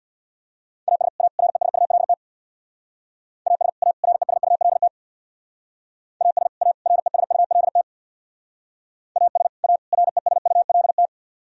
Morse code, heard as a keyed tone, about 62 wpm, 700 Hz. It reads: AS A RESULT AS A RESULT AS A RESULT AS A RESULT
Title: Speed-Racing - Sets of 3 Words 50wpm